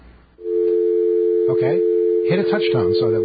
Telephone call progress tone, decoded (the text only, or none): dial tone